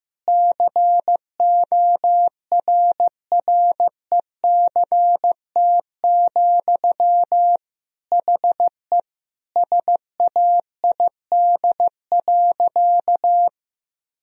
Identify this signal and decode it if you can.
Morse code, heard as a keyed tone, about 16 wpm, 700 Hz. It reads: CORRECT, HE SAID.
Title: Sentences from Top 1000 Words - Review Part 2 - 15wpm